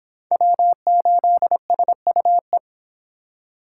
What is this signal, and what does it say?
Morse code, heard as a keyed tone, about 26 wpm, 700 Hz. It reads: W8SUE